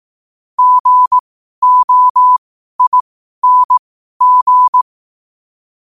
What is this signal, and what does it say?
Morse code, heard as a keyed tone, about 18 wpm, spaced as for 12 wpm, 1000 Hz. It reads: GOING
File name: qotdfiles / 12wpm